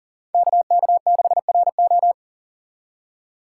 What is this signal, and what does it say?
Morse code, heard as a keyed tone, about 37 wpm, 700 Hz. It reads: KK6RO
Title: USA Call Signs - 5 Characters - 4x Character Spacing - 40wpm